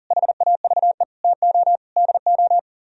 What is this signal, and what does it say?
Morse code, heard as a keyed tone, about 52 wpm, 700 Hz. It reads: HAVE TO DO